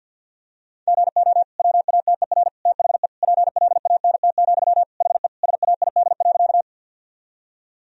Morse code, heard as a keyed tone, about 50 wpm, 700 Hz. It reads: GO WATER THE PLANT, HE SAID.
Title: Sentences from Top 1000 Words - Review Part 1 - 50wpm